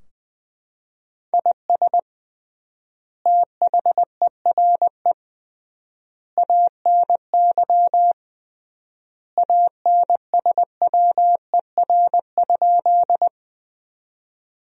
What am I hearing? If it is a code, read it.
Morse code, heard as a keyed tone, about 20 wpm, 700 Hz. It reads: IS THERE ANY ANSWER?
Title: 3x Word-Spacing - Sentences from Top 300 Words - 20wpm